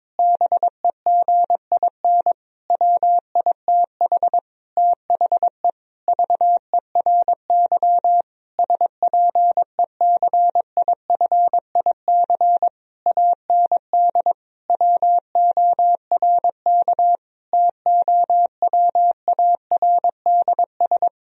Morse code, heard as a keyed tone, about 22 wpm, 700 Hz. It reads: BEGIN WITH THE VERY SPECIFIC AND WORK TOWARDS
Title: Sentences from Top 1000 Words - Review Part 3 - 22wpm